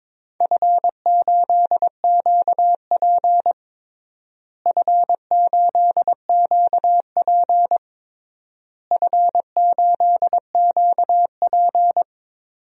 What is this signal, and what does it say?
Morse code, heard as a keyed tone, about 22 wpm, 700 Hz. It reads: F8QP F8QP F8QP
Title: DXCC-100 Call Signs - 4 Characters - Repeated 3x - 22wpm